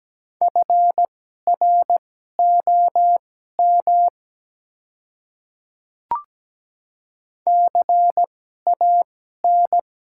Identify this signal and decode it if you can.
Morse code, heard as a keyed tone, about 17 wpm, spaced as for 12 wpm, 700 Hz. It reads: FROM CAN
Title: 2x Character Spacing - Top 100 Words - 17wpm